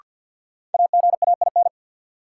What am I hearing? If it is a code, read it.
Morse code, heard as a keyed tone, about 53 wpm, 700 Hz. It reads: AGAIN
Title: Sets of 4 Words - New Word Drill - 50wpm